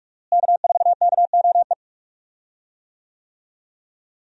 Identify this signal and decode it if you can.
Morse code, heard as a keyed tone, about 46 wpm, 700 Hz. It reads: K4KOE
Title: USA Call Signs - 5 Characters - Repeated 3x - 45wpm